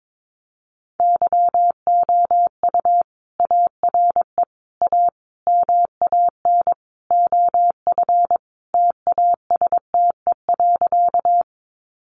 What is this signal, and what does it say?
Morse code, heard as a keyed tone, about 22 wpm, 700 Hz. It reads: YOU ARE A MAN OF TASTE.
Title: Sentences from Top 1100 Words 22wpm